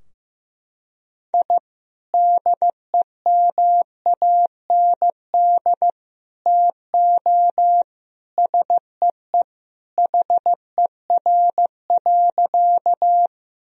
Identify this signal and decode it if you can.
Morse code, heard as a keyed tone, about 15 wpm, 700 Hz. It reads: I DEMAND TO SEE HER.
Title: Sentences from Top 1000 Words - Review Part 4 - 15wpm